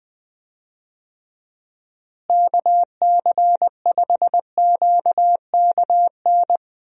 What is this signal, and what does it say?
Morse code, heard as a keyed tone, about 20 wpm, 700 Hz. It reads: KC5QKN
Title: USA Call Signs - 6 Characters - Repeated 3x - 20wpm